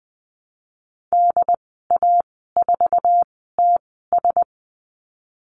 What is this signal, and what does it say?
Morse code, heard as a keyed tone, about 20 wpm, spaced as for 14 wpm, 700 Hz. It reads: DA4TS
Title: 069 Lesson 12g - Letter d - Call Signs 20wpm